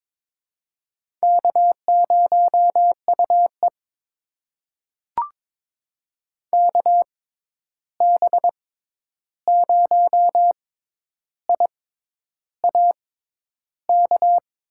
Morse code, heard as a keyed tone, about 22 wpm, spaced as for 10 wpm, 700 Hz. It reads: K0UE KB0IAK
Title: USA Call Signs - 6x Character Spacing - 22wpm